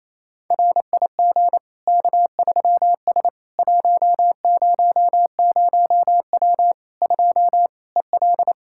Morse code, heard as a keyed tone, about 28 wpm, 700 Hz. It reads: RIG K3S 100W 2 EL
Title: Practice QSO - Exchange 2 - Info - 28wpm